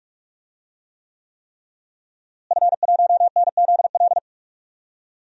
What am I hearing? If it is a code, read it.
Morse code, heard as a keyed tone, about 41 wpm, 700 Hz. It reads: F1NZL
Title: DXCC-100 Call Signs - 5 Characters - Repeated 3x - 45wpm